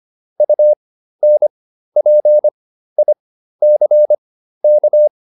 Morse code, heard as a keyed tone, about 25 wpm, spaced as for 13 wpm, 600 Hz. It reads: UNPICK